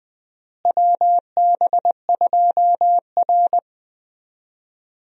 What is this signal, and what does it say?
Morse code, heard as a keyed tone, about 20 wpm, 700 Hz. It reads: WB2R
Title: USA Call Signs - 4 Characters - Repeated 2x - 20wpm